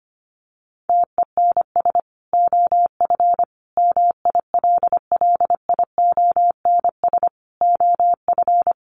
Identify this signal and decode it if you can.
Morse code, heard as a keyed tone, about 25 wpm, 700 Hz. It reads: TENS OF MILLIONS OF
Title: Sets of 5 Words Encore 25wpm